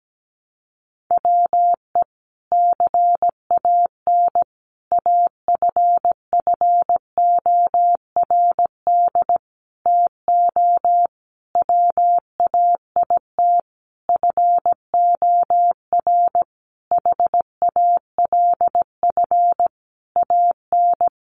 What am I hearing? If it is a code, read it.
Morse code, heard as a keyed tone, about 17 wpm, 700 Hz. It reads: WE CAN AFFORD TO WAIT FOR HALF AN
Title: Sentences from Top 1100 Words 17wpm